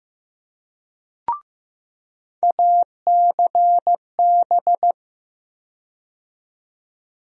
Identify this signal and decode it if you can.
Morse code, heard as a keyed tone, about 15 wpm, 700 Hz. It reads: ACB
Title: USA Call Sign Postfixes - 3 Letters 15wpm